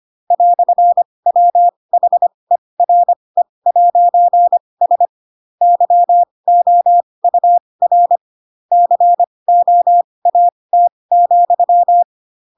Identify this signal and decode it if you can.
Morse code, heard as a keyed tone, about 25 wpm, spaced as for 20 wpm, 700 Hz. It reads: "WHERE'S YOUR COAT,
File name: Pinocchio / Pinocchio ch8 15 25 ik1hge